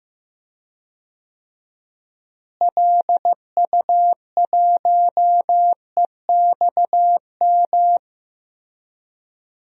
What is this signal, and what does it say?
Morse code, heard as a keyed tone, about 15 wpm, 700 Hz. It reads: LU1EXM
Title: DXCC-100 Call Signs - 6 Characters - Repeated 3x - 15wpm